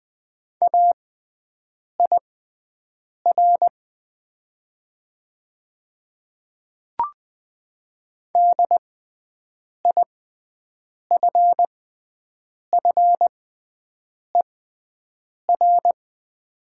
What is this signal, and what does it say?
Morse code, heard as a keyed tone, about 20 wpm, spaced as for 7 wpm, 700 Hz. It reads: AIR DIFFER
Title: 6x Character Spacing - Top 200 Words - 20wpm